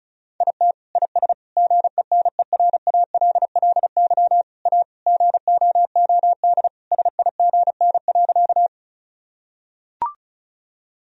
Morse code, heard as a keyed tone, about 34 wpm, 700 Hz. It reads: IT IS GENERALLY A GOOD SIGN.
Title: Sentences from Top 1000 Words - Review Part 4 - 35wpm